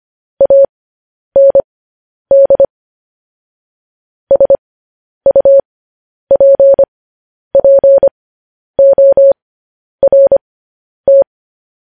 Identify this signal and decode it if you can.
Morse code, heard as a keyed tone, about 25 wpm, spaced as for 10 wpm, 550 Hz. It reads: AND SUPPORT
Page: news headlines in morse code at 10 wpm